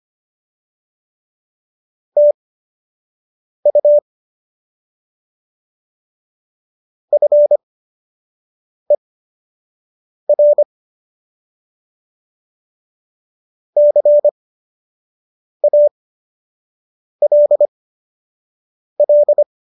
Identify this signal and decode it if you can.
Morse code, heard as a keyed tone, about 25 wpm, spaced as for 6 wpm, 590 Hz. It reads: TU FER CALL